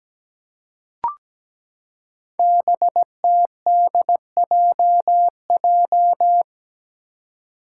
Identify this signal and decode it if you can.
Morse code, heard as a keyed tone, about 17 wpm, 700 Hz. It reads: BTDJJ
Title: Five Challenging Letter-Number Combinations 17wpm